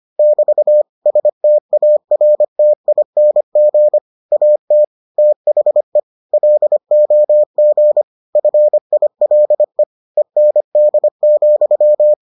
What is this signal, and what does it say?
Morse code, heard as a keyed tone, about 25 wpm, 600 Hz. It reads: <BT> STARTING AT THE LOG FILE END,